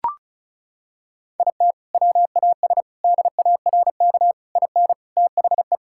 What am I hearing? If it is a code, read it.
Morse code, heard as a keyed tone, about 36 wpm, 700 Hz. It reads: IT WAS DARK IN THE
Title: Sentences from Top 1600 Words 35wpm